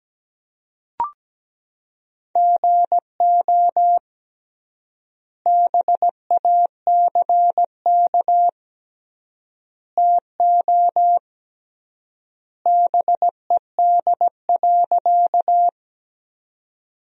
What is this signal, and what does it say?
Morse code, heard as a keyed tone, about 17 wpm, 700 Hz. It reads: GO BACK TO BED.
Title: 3x Word-Spacing - Sentences from Top 500 Words - 17wpm